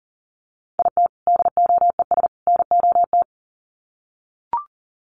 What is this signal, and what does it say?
Morse code, heard as a keyed tone, about 40 wpm, 700 Hz. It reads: IT DOES NOT
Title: Sets of 3 Words 40wpm